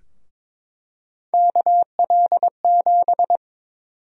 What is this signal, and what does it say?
Morse code, heard as a keyed tone, about 21 wpm, 700 Hz. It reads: KL7
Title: USA Call Sign Prefixes - Repeated 2x - 22wpm